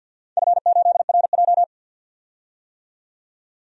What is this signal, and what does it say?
Morse code, heard as a keyed tone, about 51 wpm, 700 Hz. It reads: F8RJ
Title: DXCC-100 Call Signs - 4 Characters - 50wpm